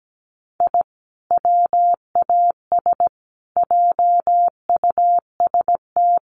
Morse code, heard as a keyed tone, about 17 wpm, 700 Hz. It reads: I WAS JUST
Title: Sets of 5 Words Encore 17wpm